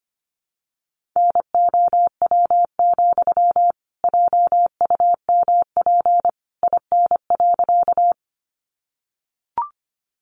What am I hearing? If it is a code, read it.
Morse code, heard as a keyed tone, about 25 wpm, 700 Hz. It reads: NOW, JUMP IN.
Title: Sentences from Top 1300 Words 25wpm